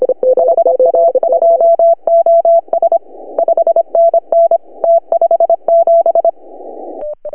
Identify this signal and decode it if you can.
Morse code, heard as a keyed tone, about 25 wpm, 660 Hz. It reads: HA2OS 5NN T57
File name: CW / HA2OS RHOB 2017 CW